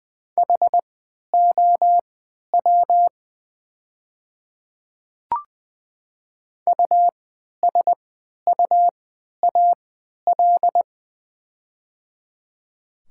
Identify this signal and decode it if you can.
Morse code, heard as a keyed tone, about 20 wpm, spaced as for 11 wpm, 700 Hz. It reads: HOW USUAL
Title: 3x Character Spacing - Top 500 Words - 20wpm